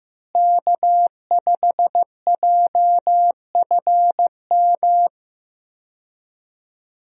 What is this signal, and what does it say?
Morse code, heard as a keyed tone, about 15 wpm, 700 Hz. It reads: K5JFM
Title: USA Call Signs - 5 Characters - Repeated 3x - 15wpm